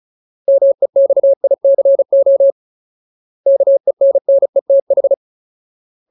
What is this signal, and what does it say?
Morse code, heard as a keyed tone, about 34 wpm, 550 Hz. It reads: MEXICO KENNETH
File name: Ultimate Word List / 35WPM Ultimate